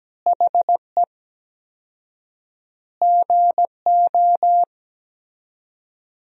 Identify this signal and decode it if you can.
Morse code, heard as a keyed tone, about 17 wpm, 700 Hz. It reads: HE GO
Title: 4x Word-Spacing - Sentences from Top 500 Words - 17wpm